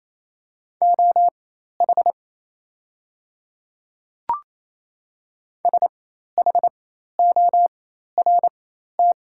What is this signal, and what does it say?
Morse code, heard as a keyed tone, about 27 wpm, spaced as for 13 wpm, 700 Hz. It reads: OH SHORT
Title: 4x Character Spacing - Top 500 Words - 28wpm